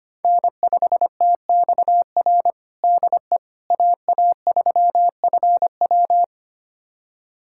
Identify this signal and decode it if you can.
Morse code, heard as a keyed tone, about 25 wpm, 700 Hz. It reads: N5TXR DE AA3FW
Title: USA Callsign DE Callsign - 5 Characters - 25wpm